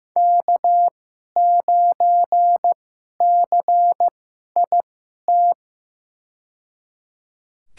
Morse code, heard as a keyed tone, about 15 wpm, spaced as for 11 wpm, 700 Hz. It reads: K9CIT
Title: USA Call Signs - 5 Characters - 2x Character Spacing - 15wpm